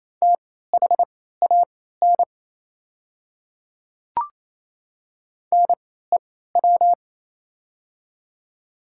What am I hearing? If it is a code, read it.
Morse code, heard as a keyed tone, about 28 wpm, spaced as for 16 wpm, 700 Hz. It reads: THAN NEW
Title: 3x Character Spacing - Top 300 Words - 28wpm